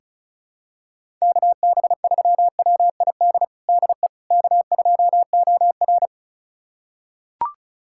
Morse code, heard as a keyed tone, about 36 wpm, 700 Hz. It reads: KB3WID DE K2OR
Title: USA Callsign DE Callsign 35wpm